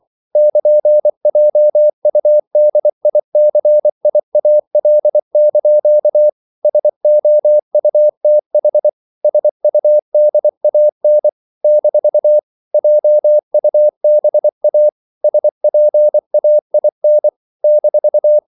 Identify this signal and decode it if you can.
Morse code, heard as a keyed tone, about 24 wpm, 600 Hz. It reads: <KN>JUDICIAL) SOUTH SUDAN - JUBA SPAIN -
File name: CW / Practice / NationalCapitals24